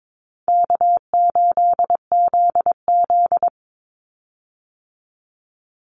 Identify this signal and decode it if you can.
Morse code, heard as a keyed tone, about 22 wpm, 700 Hz. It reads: K8ZZ
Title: Easy Call Signs Repeated 2x - 22wpm